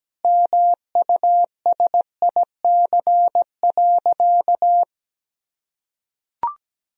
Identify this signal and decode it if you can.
Morse code, heard as a keyed tone, about 17 wpm, 700 Hz. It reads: MUSIC.
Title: Sentences from Top 1000 Words - Review Part 3 - 17wpm